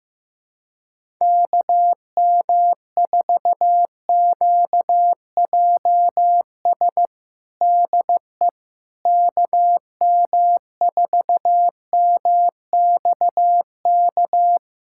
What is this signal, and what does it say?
Morse code, heard as a keyed tone, about 15 wpm, 700 Hz. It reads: KM4QJS DE KM4MXK
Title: USA Callsign DE Callsign - 6 Characters - 15wpm